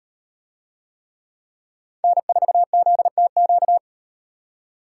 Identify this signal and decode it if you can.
Morse code, heard as a keyed tone, about 37 wpm, 700 Hz. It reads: N4ZTQ